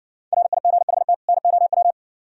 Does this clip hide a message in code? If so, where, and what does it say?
Yes: Morse code from 0.3 to 1.9 s, heard as a keyed tone, about 60 wpm, 700 Hz. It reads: RIGHT NOW